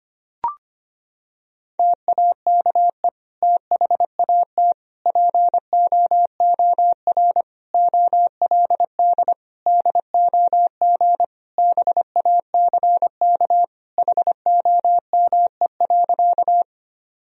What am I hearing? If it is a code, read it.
Morse code, heard as a keyed tone, about 25 wpm, 700 Hz. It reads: TAKE THAT POOR OLD DOG BACK HOME.
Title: Sentences from Top 1000 Words - Review Part 2 - 25wpm